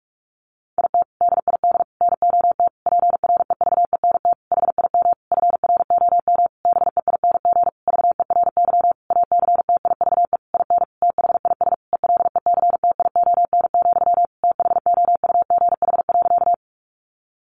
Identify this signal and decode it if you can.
Morse code, heard as a keyed tone, about 44 wpm, 700 Hz. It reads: IT DID NOT PREVENT HIM FROM BEING VERY ACTIVE IN THIS ELECTION, THOUGH.
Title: Sentences from Top 1500 Words 45wpm